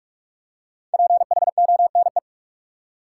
Morse code, heard as a keyed tone, about 46 wpm, 700 Hz. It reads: PHONE
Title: Five Letter Words - 45wpm